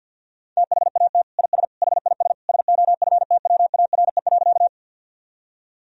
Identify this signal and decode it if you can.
Morse code, heard as a keyed tone, about 50 wpm, 700 Hz. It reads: THAT IS HIS SOFTWARE.
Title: Sentences from Top 1000 Words - Review Part 3 - 50wpm